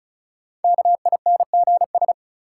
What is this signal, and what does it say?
Morse code, heard as a keyed tone, about 36 wpm, 700 Hz. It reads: KINGS
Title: Sentences from Top 1000 Words - Review Part 6 - New Word Drill - 35wpm